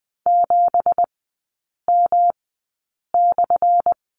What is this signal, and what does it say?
Morse code, heard as a keyed tone, about 20 wpm, spaced as for 8 wpm, 700 Hz. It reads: 7M/
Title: 178 Lesson 29e - Slash - Three Characters 20wpm